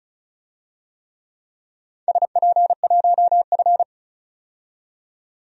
Morse code, heard as a keyed tone, about 34 wpm, 700 Hz. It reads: SP1F
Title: DXCC-100 Call Signs - 4 Characters - 35wpm